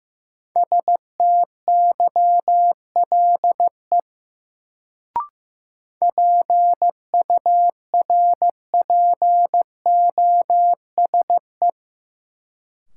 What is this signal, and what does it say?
Morse code, heard as a keyed tone, about 15 wpm, 700 Hz. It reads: STYLE PURPOSE
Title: Sentences from Top 800 Words - New Word Drill - 15wpm